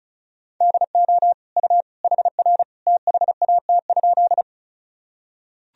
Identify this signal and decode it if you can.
Morse code, heard as a keyed tone, about 35 wpm, 700 Hz. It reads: DO U HR THAT?
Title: Sentences from Top 300 Words and Abbreviations 35wpm